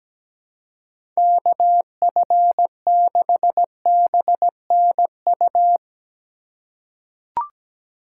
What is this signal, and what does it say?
Morse code, heard as a keyed tone, about 17 wpm, 700 Hz. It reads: KF6BNU
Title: USA Call Signs - 6 Characters - 2x Character Spacing - 17wpm